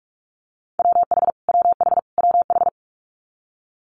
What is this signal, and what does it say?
Morse code, heard as a keyed tone, about 46 wpm, 700 Hz. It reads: WH WH WH
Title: Multi-letter Phonemes Repeated 3x 45wpm